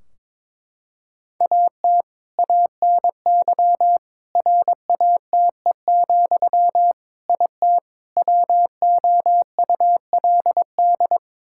Morse code, heard as a keyed tone, about 22 wpm, 700 Hz. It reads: AT ANY RATE, IT WOULD
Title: Sentences from Top 1600 Words 22wpm